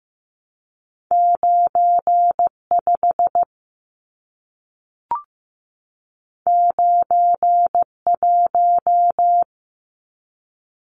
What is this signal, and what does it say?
Morse code, heard as a keyed tone, about 15 wpm, 700 Hz. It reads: 9 5 9 1
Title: Numbers - 2-Digit 15wpm